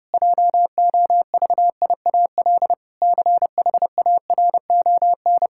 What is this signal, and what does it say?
Morse code, heard as a keyed tone, about 32 wpm, 700 Hz. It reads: JOVIAL CHARON
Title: Pangram Challenge v2 - 30wpm